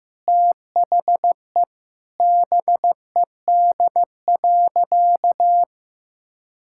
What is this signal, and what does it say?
Morse code, heard as a keyed tone, about 15 wpm, 700 Hz. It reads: THE BED.
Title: Sentences from Top 1000 Words - Review Part 5 - 15wpm